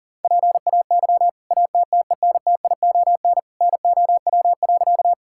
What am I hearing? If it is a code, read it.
Morse code, heard as a keyed tone, about 40 wpm, 700 Hz. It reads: PAY ATTENTION NOW.
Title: Sentences from Top 1000 Words - Review Part 6 - 40wpm